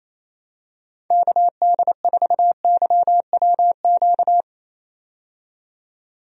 Morse code, heard as a keyed tone, about 27 wpm, 700 Hz. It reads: KD4YWQ
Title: USA Call Signs - 6 Characters - 28wpm